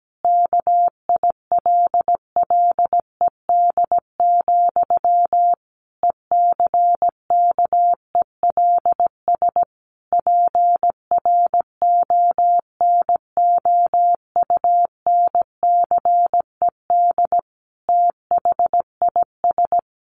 Morse code, heard as a keyed tone, about 17 wpm, 700 Hz. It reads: KILLED, ECKELS PRONOUNCED THIS
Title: A Sound Of Thunder 17wpm